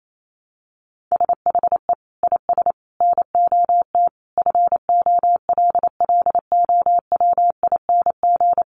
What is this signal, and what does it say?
Morse code, heard as a keyed tone, about 28 wpm, 700 Hz. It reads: SHE IS NOT FOLLOWING
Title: Sentences from Top 700 Words 28wpm